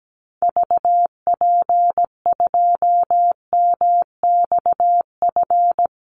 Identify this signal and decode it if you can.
Morse code, heard as a keyed tone, about 17 wpm, 700 Hz. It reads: VP2MXF
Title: Call Signs Repeated 3x - 17wpm